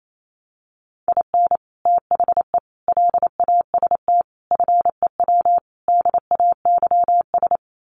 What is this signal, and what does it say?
Morse code, heard as a keyed tone, about 28 wpm, 700 Hz. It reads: IN THE LAST FEW DAYS
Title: Sets of 5 Words Encore2 28wpm